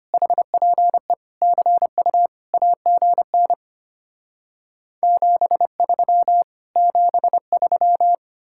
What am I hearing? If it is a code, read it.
Morse code, heard as a keyed tone, about 33 wpm, 700 Hz. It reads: HPE CU AGN 73 73